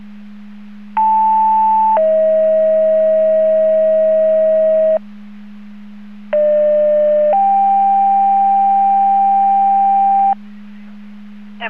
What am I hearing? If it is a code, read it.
Two-tone page, heard as a one-second tone, then a three-second tone, about 880 then 640 Hz, about 600 then 790 Hz. It is heard from 1.0 to 10.3 s.